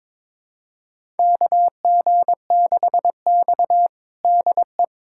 Morse code, heard as a keyed tone, about 21 wpm, 700 Hz. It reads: KG6X DE
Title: USA Callsign DE Callsign - 4 Characters - 22wpm